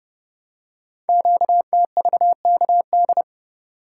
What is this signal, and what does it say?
Morse code, heard as a keyed tone, about 30 wpm, 700 Hz. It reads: QTVKD
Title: Code Groups - 5 Letter:Numbers - 30wpm